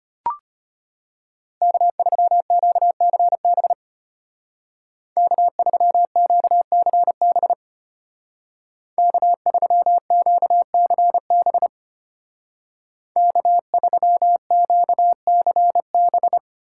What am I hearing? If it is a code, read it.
Morse code, heard as a keyed tone, about 33 wpm, 700 Hz. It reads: K3QCB K3QCB K3QCB K3QCB